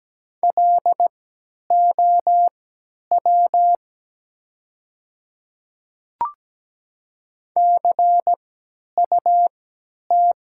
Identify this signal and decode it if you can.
Morse code, heard as a keyed tone, about 17 wpm, spaced as for 10 wpm, 700 Hz. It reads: LOW CUT
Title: 3x Character Spacing - Top 300 Words - 17wpm